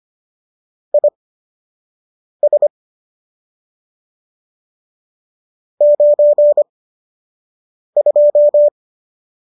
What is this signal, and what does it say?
Morse code, heard as a keyed tone, about 25 wpm, spaced as for 6 wpm, 590 Hz. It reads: IS 92